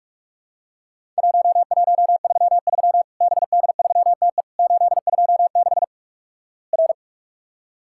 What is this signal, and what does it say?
Morse code, heard as a keyed tone, about 46 wpm, 700 Hz. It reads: 1133 BD3TE 826 R
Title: Worked All Europe DX Contest - QTC Exchange - 45wpm